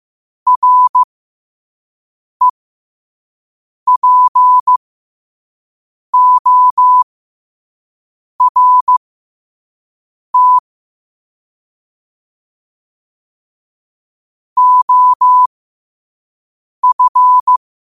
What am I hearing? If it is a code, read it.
Morse code, heard as a keyed tone, about 15 wpm, spaced as for 5 wpm, 1000 Hz. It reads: REPORT OF
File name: qotdfiles / 5wpm